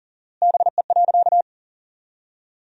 Morse code, heard as a keyed tone, about 40 wpm, 700 Hz. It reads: BE.
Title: 4x Word-Spacing - Sentences from Top 200 Words - 40wpm